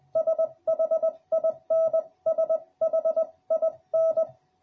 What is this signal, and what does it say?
Morse code, heard as a keyed tone, about 20 wpm, 650 Hz. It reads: SHIN SHIN